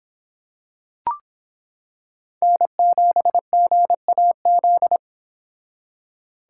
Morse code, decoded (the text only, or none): N7GAZ